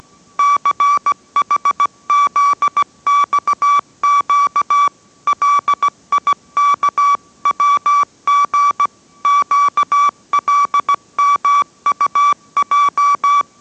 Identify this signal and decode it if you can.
Morse code, heard as a keyed tone, about 17 wpm, 1150 Hz. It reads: CHZXQ LIKWG QLMUJ